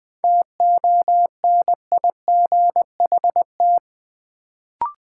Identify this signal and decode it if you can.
Morse code, heard as a keyed tone, about 20 wpm, 700 Hz. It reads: TONIGHT
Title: Sentences from Top 600 Words - New Word Drill - 20wpm